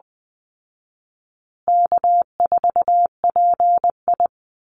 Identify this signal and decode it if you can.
Morse code, heard as a keyed tone, about 20 wpm, 700 Hz. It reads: K4PI
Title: Call Signs Repeated 3x - 20wpm